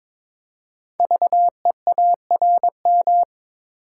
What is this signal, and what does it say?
Morse code, heard as a keyed tone, about 22 wpm, 700 Hz. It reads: VEARM
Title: Code Groups - 5 Letters - 22wpm